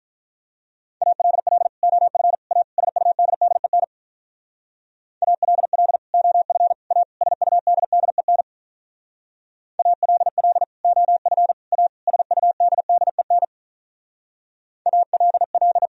Morse code, heard as a keyed tone, about 46 wpm, 700 Hz. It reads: ALL OF A SUDDEN ALL OF A SUDDEN ALL OF A SUDDEN ALL